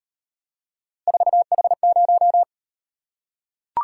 Morse code, heard as a keyed tone, about 36 wpm, 700 Hz. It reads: 4H0